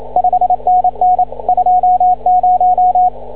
Morse code, heard as a keyed tone, about 28 wpm, 730 Hz, with radio noise. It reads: 5NN 20
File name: CW / ha6oi 2